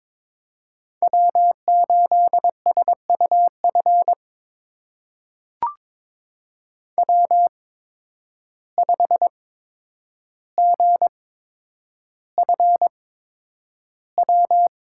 Morse code, heard as a keyed tone, about 22 wpm, spaced as for 9 wpm, 700 Hz. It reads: W8SUF W5GFW
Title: USA Call Signs - 5 Characters - 8x Character Spacing - 22wpm